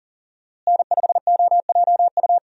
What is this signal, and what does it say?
Morse code, heard as a keyed tone, about 40 wpm, 700 Hz. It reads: N5OJU
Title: USA Call Signs - 5 Characters - Repeated 2x - 40wpm